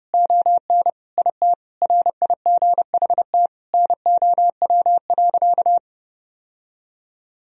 Morse code, heard as a keyed tone, about 32 wpm, 700 Hz. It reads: ON IT RIGHT NOW.
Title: Sentences from Top 1000 Words - Review Part 4 - 30wpm